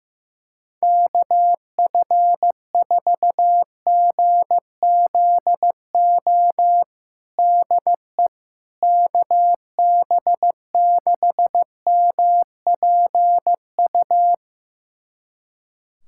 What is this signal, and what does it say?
Morse code, heard as a keyed tone, about 16 wpm, 700 Hz. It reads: KF4GZO DE KB6MPU
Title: USA Callsign DE Callsign - 6 Characters - 15wpm